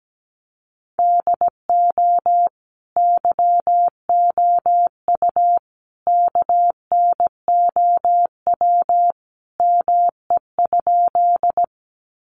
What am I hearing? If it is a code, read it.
Morse code, heard as a keyed tone, about 17 wpm, 700 Hz. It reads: DO YOU KNOW ME?
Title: Sentences from Top 200 Words 17wpm